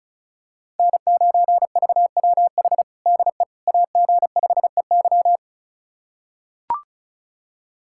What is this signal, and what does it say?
Morse code, heard as a keyed tone, about 36 wpm, 700 Hz. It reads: N9VWH DE AG5EY
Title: USA Callsign DE Callsign - 5 Characters - 35wpm